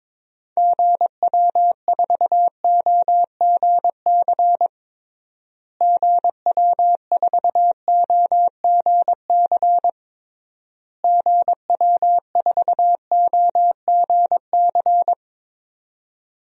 Morse code, heard as a keyed tone, about 22 wpm, 700 Hz. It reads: GW4OGC GW4OGC GW4OGC
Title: DXCC-100 Call Signs - 6 Characters - Repeated 3x - 22wpm